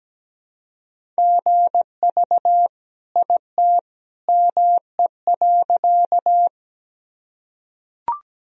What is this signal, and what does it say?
Morse code, heard as a keyed tone, about 17 wpm, 700 Hz. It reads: GV IT ME.
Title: Sentences from Top 200 Words and Abbreviations 17wpm